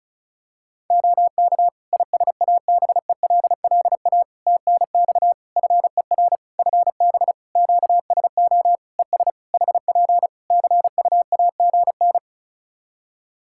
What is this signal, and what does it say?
Morse code, heard as a keyed tone, about 36 wpm, 700 Hz. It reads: OK ISABELLA TNX FER FB QSO ES HP CUAGN
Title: Practice QSO - Exchange 4 - Ending - 35wpm